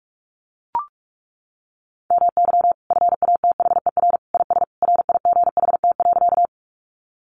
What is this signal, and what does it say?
Morse code, heard as a keyed tone, about 44 wpm, 700 Hz. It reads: MY FATHER IS RIGHT.
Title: Sentences from Top 300 Words 45wpm